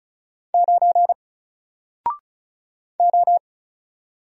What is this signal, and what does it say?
Morse code, heard as a keyed tone, about 34 wpm, 700 Hz. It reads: 9 O
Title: Single Letter-Number Alt 35wpm